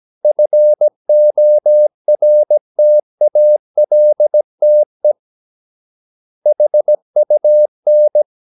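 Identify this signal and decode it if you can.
Morse code, heard as a keyed tone, about 17 wpm, 600 Hz. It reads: FORTALTE HUN
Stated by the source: Slow (QRS) Morse practice at 17 wpm, from CW med Gnister.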